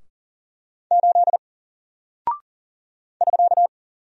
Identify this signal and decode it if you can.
Morse code, heard as a keyed tone, about 39 wpm, 700 Hz. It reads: 8 <SK>
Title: Single Character (Letter:Number:Punctuation:Prosign) Alt 40wpm